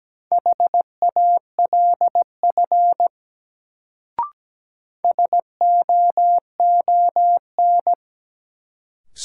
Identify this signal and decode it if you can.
Morse code, heard as a keyed tone, about 17 wpm, 700 Hz. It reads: HALF SOON
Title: Sets of 3 Words - New Word Drill - 17wpm